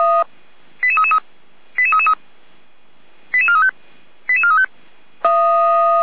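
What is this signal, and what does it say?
Five-tone selective call (ZVEI), heard as quick runs of five tones, tones of about 80 ms. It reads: 88333 88333 88435 88435
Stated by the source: ZVEI five-tone call received by SDR, from feuerwehr-orlamuende.org archive.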